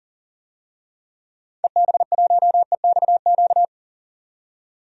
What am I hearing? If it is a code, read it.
Morse code, heard as a keyed tone, about 40 wpm, 700 Hz. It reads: EB1EXQ
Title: DXCC-100 Call Signs - 6 Characters - 40wpm